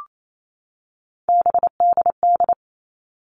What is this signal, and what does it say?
Morse code, heard as a keyed tone, about 28 wpm, 700 Hz. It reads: BDD
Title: All Cut Numbers - 3-Digit 28wpm